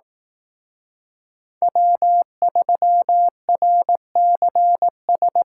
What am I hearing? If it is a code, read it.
Morse code, heard as a keyed tone, about 18 wpm, 700 Hz. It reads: W3RCS
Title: Speed Racing - USA 1x3 Call Signs 15wpm